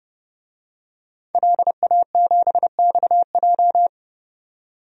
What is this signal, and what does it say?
Morse code, heard as a keyed tone, about 32 wpm, 700 Hz. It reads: LA7XJ
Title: DXCC-100 Call Signs - 5 Characters - 30wpm